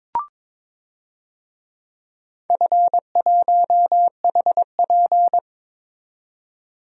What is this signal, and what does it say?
Morse code, heard as a keyed tone, about 22 wpm, 700 Hz. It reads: F1HP
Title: DXCC-100 Call Signs - 4 Characters - 22wpm